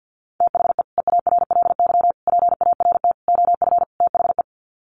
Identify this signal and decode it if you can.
Morse code, heard as a keyed tone, about 50 wpm, 700 Hz. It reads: THE EARLY PART OF THE
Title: Sets of 5 Words Encore2 50wpm